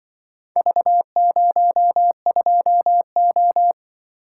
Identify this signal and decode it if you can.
Morse code, heard as a keyed tone, about 24 wpm, 700 Hz. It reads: V02O
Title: Speed-Racing - Four Letter-Number Combinations - 20wpm